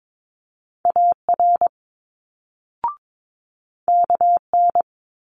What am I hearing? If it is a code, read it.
Morse code, heard as a keyed tone, about 22 wpm, 700 Hz. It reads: AR KN
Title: Multi-letter Phonemes Quick Fire 22wpm